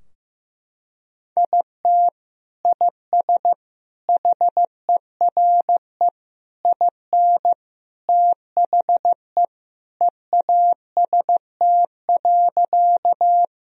Morse code, heard as a keyed tone, about 15 wpm, 700 Hz. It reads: IT IS HERE IN THE EAST.
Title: Sentences from Top 1000 Words - Review Part 3 - 15wpm